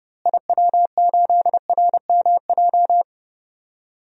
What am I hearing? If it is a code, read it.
Morse code, heard as a keyed tone, about 30 wpm, 700 Hz. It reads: IW8RMJ
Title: DXCC-100 Call Signs - 6 Characters - Repeated 2x - 30wpm